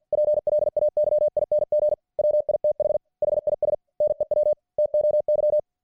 Morse code, heard as a keyed tone, about 50 wpm, 600 Hz. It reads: PLAYING WITH HIS NEW TOY